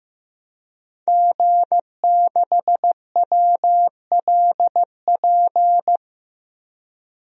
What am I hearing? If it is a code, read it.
Morse code, heard as a keyed tone, about 15 wpm, 700 Hz. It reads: G6WLP
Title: DXCC-100 Call Signs - 5 Characters - Repeated 2x - 15wpm